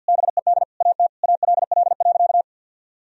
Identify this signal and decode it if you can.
Morse code, heard as a keyed tone, about 50 wpm, 700 Hz. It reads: BED AT ALL.
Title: Sentences from Top 1000 Words - Review Part 3 - 50wpm